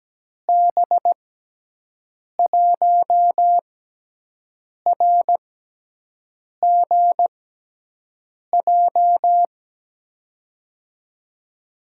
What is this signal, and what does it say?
Morse code, heard as a keyed tone, about 17 wpm, spaced as for 6 wpm, 700 Hz. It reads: B1RGJ
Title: USA Call Signs - 6 Characters - 6x Character Spacing - 17wpm